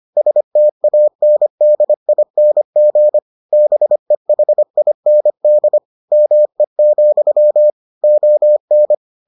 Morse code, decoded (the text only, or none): STANDING BEHIND ME, ON